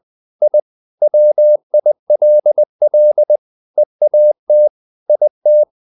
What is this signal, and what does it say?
Morse code, heard as a keyed tone, about 20 wpm, 600 Hz. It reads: I WILL EAT IT
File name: the old man and the sea / the old man and the sea part 1 20wpm